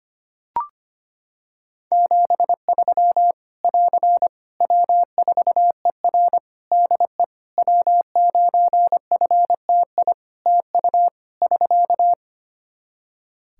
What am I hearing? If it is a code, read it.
Morse code, heard as a keyed tone, about 25 wpm, 700 Hz. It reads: 73 <AR> W4ER DE W9FTI TU <SK>
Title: Practice QSO - Exchange 4 - Ending - 25wpm